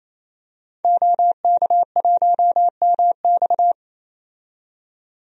Morse code, heard as a keyed tone, about 28 wpm, 700 Hz. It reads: OK1MX
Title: DXCC-100 Call Signs - 5 Characters - 28wpm